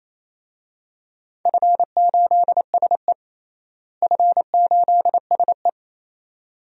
Morse code, heard as a keyed tone, about 27 wpm, 700 Hz. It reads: F8SE F8SE
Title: DXCC-100 Call Signs - 4 Characters - Repeated 2x - 28wpm